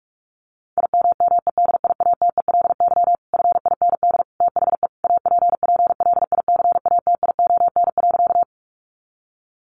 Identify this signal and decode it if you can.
Morse code, heard as a keyed tone, about 44 wpm, 700 Hz. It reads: IMMEDIATELY FIND THE APPLICATION.
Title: Sentences from Top 1200 Words 45wpm